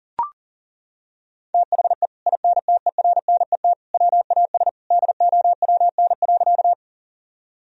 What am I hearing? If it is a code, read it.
Morse code, heard as a keyed tone, about 40 wpm, 700 Hz. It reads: THE INTERNET WAS DOWN.
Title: Sentences from Top 1000 Words - Review Part 4 - 40wpm